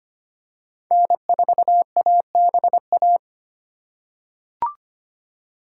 Morse code, heard as a keyed tone, about 25 wpm, 700 Hz. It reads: N4ABA
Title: USA Call Signs - 5 Characters - 3x Character Spacing - 25wpm